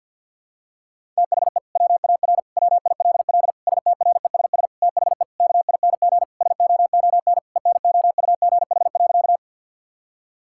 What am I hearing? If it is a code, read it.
Morse code, heard as a keyed tone, about 46 wpm, 700 Hz. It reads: THE WAR WILL STRESS THE KING SOON ENOUGH.
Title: Sentences from Top 1000 Words - Review Part 6 - 50wpm